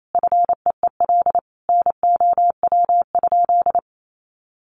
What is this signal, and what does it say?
Morse code, heard as a keyed tone, about 28 wpm, 700 Hz. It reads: FEEL NOW?
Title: Sentences from Top 400 Words 28wpm